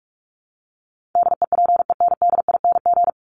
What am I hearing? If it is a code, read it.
Morse code, heard as a keyed tone, about 44 wpm, 700 Hz. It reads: DEPENDING
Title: ING Suffix - Top 100 Words with Base Words 45wpm